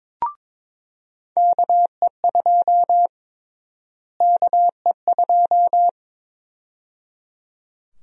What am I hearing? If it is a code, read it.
Morse code, heard as a keyed tone, about 22 wpm, 700 Hz. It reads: KE2 KE2
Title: USA Call Sign Prefixes - Repeated 2x - 22wpm